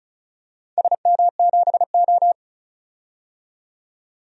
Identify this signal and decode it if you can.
Morse code, heard as a keyed tone, about 36 wpm, 700 Hz. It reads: SM7O
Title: DXCC-100 Call Signs - 4 Characters - 35wpm